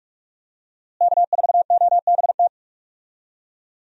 Morse code, heard as a keyed tone, about 42 wpm, 700 Hz. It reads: K4OBT
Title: USA Call Signs - 5 Characters - Repeated 3x - 45wpm